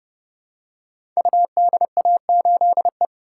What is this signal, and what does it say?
Morse code, heard as a keyed tone, about 30 wpm, 700 Hz. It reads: UDA8E
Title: Code Groups - 5 Letter:Numbers - 30wpm